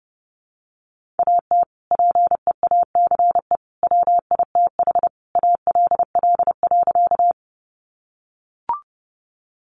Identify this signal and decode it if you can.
Morse code, heard as a keyed tone, about 30 wpm, 700 Hz. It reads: AT PEACE WITH ALL.
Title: Sentences from Top 1500 Words 30wpm